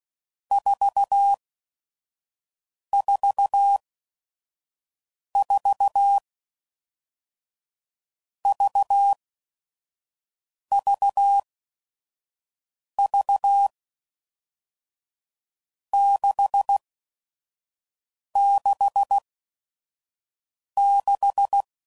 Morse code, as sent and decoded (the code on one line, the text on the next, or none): ....- / ....- / ....- / ...- / ...- / ...- / -.... / -.... / -....
4 4 4 V V V 6 6 6